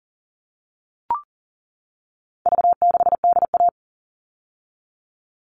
Morse code, heard as a keyed tone, about 40 wpm, 700 Hz. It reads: V6DA
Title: All Cut Numbers - 4-Digit 40wpm